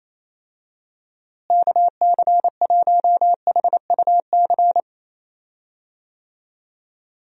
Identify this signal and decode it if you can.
Morse code, heard as a keyed tone, about 28 wpm, 700 Hz. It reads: KC1HUC
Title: USA Call Signs - 6 Characters - Repeated 2x - 28wpm